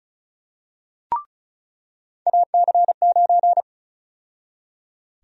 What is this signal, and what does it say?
Morse code, heard as a keyed tone, about 34 wpm, 700 Hz. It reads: AC9
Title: USA Call Sign Prefixes - 35wpm